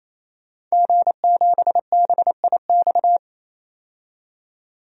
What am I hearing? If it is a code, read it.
Morse code, heard as a keyed tone, about 28 wpm, 700 Hz. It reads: G7BIX
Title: DXCC-100 Call Signs - 5 Characters - 28wpm